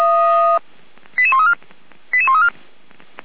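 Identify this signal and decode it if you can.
Five-tone selective call (ZVEI), heard as quick runs of five tones, tones of about 70 ms. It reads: 88134 88134
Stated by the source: Five-tone ZVEI selective call 88335 received by SDR, from feuerwehr-orlamuende fire service audio server.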